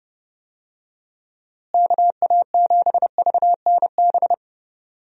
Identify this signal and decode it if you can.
Morse code, heard as a keyed tone, about 30 wpm, 700 Hz. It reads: KA7VNB
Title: USA Call Signs - 6 Characters - Repeated 3x - 30wpm